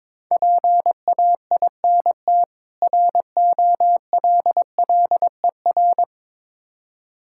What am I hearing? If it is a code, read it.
Morse code, heard as a keyed tone, about 21 wpm, 700 Hz. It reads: PAINT ROLLER
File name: Difficult Words with Hints - 22wpm